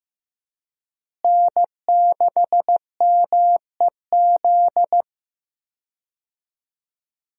Morse code, heard as a keyed tone, about 15 wpm, 700 Hz. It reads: N6MEZ
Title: USA Call Signs - 5 Characters - Repeated 2x - 15wpm